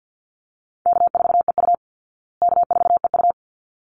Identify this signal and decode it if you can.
Morse code, heard as a keyed tone, about 50 wpm, 700 Hz. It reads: K4EU K4EU
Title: Easy Call Signs Repeated 2x - 50wpm